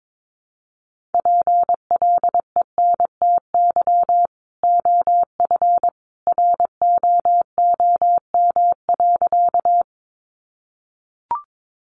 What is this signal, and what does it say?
Morse code, heard as a keyed tone, about 22 wpm, 700 Hz. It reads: PLENTY OF ROOM.
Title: Sentences from Top 1300 Words 22wpm